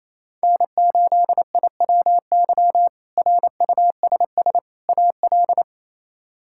Morse code, heard as a keyed tone, about 28 wpm, 700 Hz. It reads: N8IWY RUSS AL
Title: K1USN SST QSO Practice - 28wpm